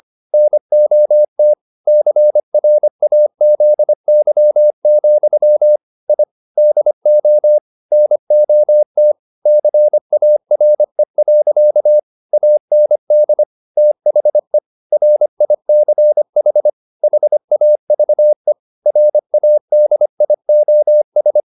Morse code, heard as a keyed tone, about 25 wpm, 600 Hz. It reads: NOT CRAZY, I DO NOT CARE. AND THE RICH HAVE RADIOS
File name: the old man and the sea part 1 25wpm